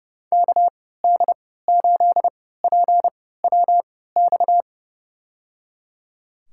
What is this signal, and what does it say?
Morse code, heard as a keyed tone, about 30 wpm, spaced as for 17 wpm, 700 Hz. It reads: KD8PWX
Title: USA Call Signs - 6 Characters - 3x Character Spacing - 30wpm